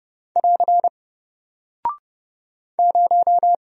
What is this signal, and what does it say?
Morse code, heard as a keyed tone, about 28 wpm, 700 Hz. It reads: <AR> 0
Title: Single Character (Letter:Number:Punctuation:Prosign) 30wpm